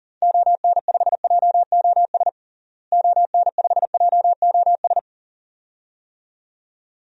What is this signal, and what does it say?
Morse code, heard as a keyed tone, about 37 wpm, 700 Hz. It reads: ON5JOS ON5JOS
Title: DXCC-100 Call Signs - 6 Characters - Repeated 2x - 40wpm